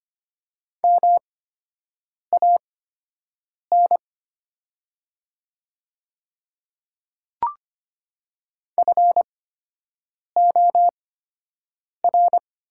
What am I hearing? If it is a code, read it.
Morse code, heard as a keyed tone, about 25 wpm, spaced as for 7 wpm, 700 Hz. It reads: MAN FOR
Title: 8x Character Spacing - Top 200 Words - 25wpm